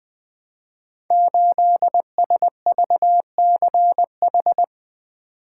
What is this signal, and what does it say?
Morse code, heard as a keyed tone, about 20 wpm, 700 Hz. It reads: 8SVCH
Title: Code Groups - 5 Letter:Numbers - 20wpm